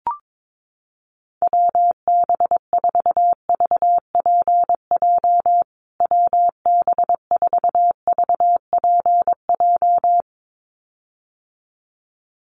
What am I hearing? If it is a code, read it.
Morse code, heard as a keyed tone, about 22 wpm, 700 Hz. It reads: WB4VPJ WB4VPJ
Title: USA Call Signs Repeated 2x 22wpm